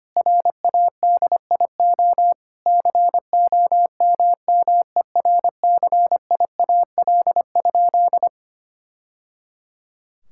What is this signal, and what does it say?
Morse code, heard as a keyed tone, about 25 wpm, 700 Hz. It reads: RADIO COMMERCIAL?
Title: Sentences from Top 1000 Words - Review Part 5 - 25wpm